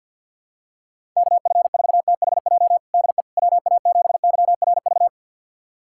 Morse code, heard as a keyed tone, about 46 wpm, 700 Hz. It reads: KF4THJ DE WA7YRV
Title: USA Callsign DE Callsign 50wpm